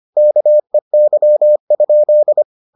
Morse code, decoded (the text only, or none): KEY?